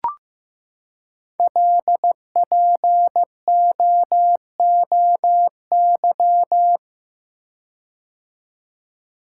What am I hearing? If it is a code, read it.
Morse code, heard as a keyed tone, about 15 wpm, 700 Hz. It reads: LPOOY
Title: Code Groups - 5 Letters - 15wpm